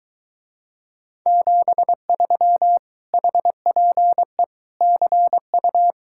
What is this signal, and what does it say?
Morse code, heard as a keyed tone, about 23 wpm, 700 Hz. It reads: 73 HPE CU